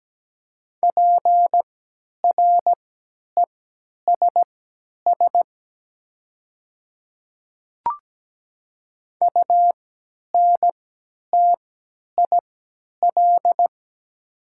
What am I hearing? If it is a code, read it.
Morse code, heard as a keyed tone, about 17 wpm, spaced as for 10 wpm, 700 Hz. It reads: PRESS UNTIL
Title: 3x Character Spacing - Top 300 Words - 17wpm